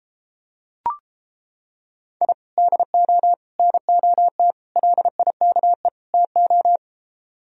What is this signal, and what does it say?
Morse code, heard as a keyed tone, about 29 wpm, 700 Hz. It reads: I DO NOT LIKE TO